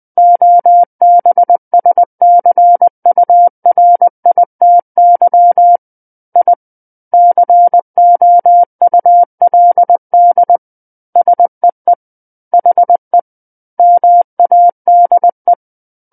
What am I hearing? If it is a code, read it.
Morse code, heard as a keyed tone, about 20 wpm, 700 Hz. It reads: OBSCURITY I COULD SEE HE MADE
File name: 20 WPM